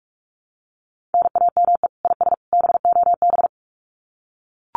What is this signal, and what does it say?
Morse code, heard as a keyed tone, about 44 wpm, 700 Hz. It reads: NAME IS BOB
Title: Common QSO Phrases 45wpm